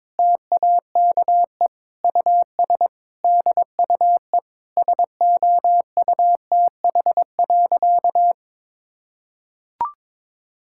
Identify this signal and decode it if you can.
Morse code, heard as a keyed tone, about 22 wpm, 700 Hz. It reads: TAKE US DUE SOUTH.
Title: Sentences from Top 1000 Words - Review Part 6 - 22wpm